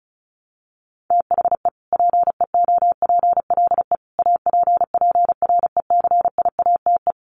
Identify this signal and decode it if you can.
Morse code, heard as a keyed tone, about 34 wpm, 700 Hz. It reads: THE PEOPLE APPRECIATE
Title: Sentences from Top 1300 Words 35wpm